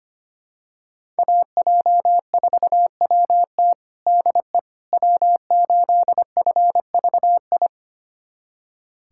Morse code, heard as a keyed tone, about 25 wpm, 700 Hz. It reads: AJ4WT DE W8FVI
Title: USA Callsign DE Callsign - 5 Characters - 25wpm